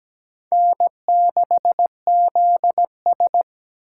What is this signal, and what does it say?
Morse code, heard as a keyed tone, about 20 wpm, 700 Hz. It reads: N6ZS
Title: Speed Racing - USA 1x2 Call Signs 17wpm